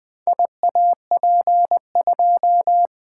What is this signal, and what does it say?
Morse code, heard as a keyed tone, about 20 wpm, 700 Hz. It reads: IAP2
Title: Four Letter-Number Combinations - 20wpm